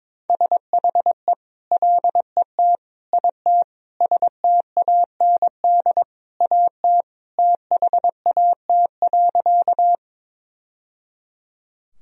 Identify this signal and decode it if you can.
Morse code, heard as a keyed tone, about 22 wpm, 700 Hz. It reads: SHE LET IT STAND AT THAT.
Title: Sentences from Top 1000 Words - Review Part 1 - 22wpm